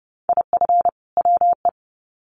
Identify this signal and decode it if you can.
Morse code, heard as a keyed tone, about 30 wpm, 700 Hz. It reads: IF WE
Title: Sets of 2 Words 30wpm